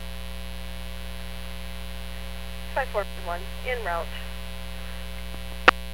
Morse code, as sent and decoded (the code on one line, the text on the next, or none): none